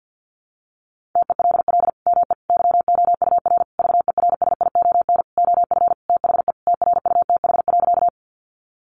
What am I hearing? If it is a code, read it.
Morse code, heard as a keyed tone, about 48 wpm, 700 Hz. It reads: TELL ME YOUR VERSION OF THE TRUTH.
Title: Sentences from Top 1400 Words 50wpm